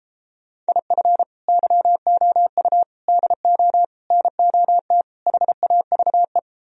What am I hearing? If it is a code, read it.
Morse code, heard as a keyed tone, about 33 wpm, 700 Hz. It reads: IF YOU DO NOT HAVE